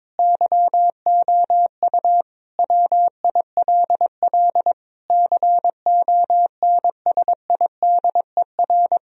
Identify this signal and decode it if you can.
Morse code, heard as a keyed tone, about 22 wpm, 700 Hz. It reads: YOU WILL CONSIDER
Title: Sentences from Top 1000 Words - Review Part 4 - 22wpm